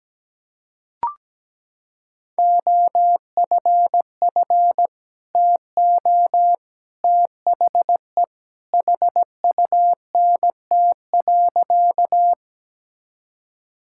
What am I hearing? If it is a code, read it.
Morse code, heard as a keyed tone, about 17 wpm, 700 Hz. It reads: OFF TO THE HUNT.
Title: Sentences from Top 1600 Words 17wpm